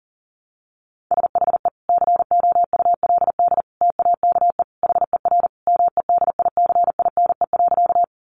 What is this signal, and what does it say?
Morse code, heard as a keyed tone, about 40 wpm, 700 Hz. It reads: SHE COULD TAKE HER MEDICINE.
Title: Sentences from Top 900 Words 40wpm